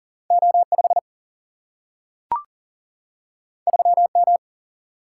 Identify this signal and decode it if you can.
Morse code, heard as a keyed tone, about 40 wpm, 700 Hz. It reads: O 5 3 M
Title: Single-Letter Single-Number 40wpm